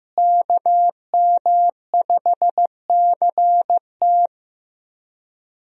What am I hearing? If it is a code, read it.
Morse code, heard as a keyed tone, about 15 wpm, 700 Hz. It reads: KM5CT
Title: USA Call Signs - 5 Characters - Repeated 2x - 15wpm